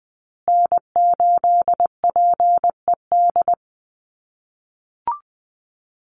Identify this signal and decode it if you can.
Morse code, heard as a keyed tone, about 20 wpm, 700 Hz. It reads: N8PED
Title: USA Call Signs Repeated 2x 20wpm